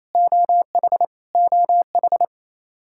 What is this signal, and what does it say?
Morse code, heard as a keyed tone, about 28 wpm, 700 Hz. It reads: OH OH
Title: Practice QSO - Exchange 1 - Beginning - 28wpm